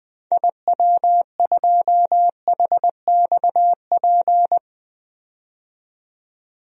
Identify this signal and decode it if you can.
Morse code, heard as a keyed tone, about 20 wpm, 700 Hz. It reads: IW2HXP
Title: DXCC-100 Call Signs - 6 Characters - 20wpm